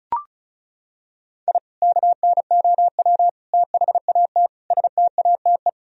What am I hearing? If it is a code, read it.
Morse code, heard as a keyed tone, about 36 wpm, 700 Hz. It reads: I KNOW THAT STATE
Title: Sentences from Top 1000 Words - Review Part 2 - 35wpm